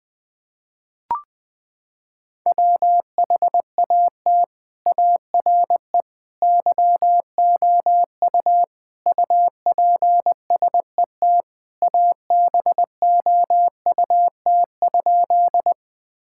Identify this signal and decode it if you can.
Morse code, heard as a keyed tone, about 21 wpm, 700 Hz. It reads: WHAT ARE YOU UPSET ABOUT?
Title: Sentences from Top 1000 Words - Review Part 5 - 20wpm